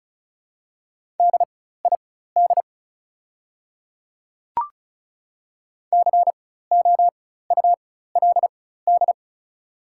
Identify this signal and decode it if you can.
Morse code, heard as a keyed tone, about 36 wpm, spaced as for 16 wpm, 690 Hz. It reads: DID COULD
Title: 4x Character Spacing - Top 100 Words - 35wpm